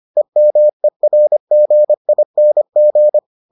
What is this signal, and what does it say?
Morse code, heard as a keyed tone, about 25 wpm, 600 Hz. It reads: EMERGING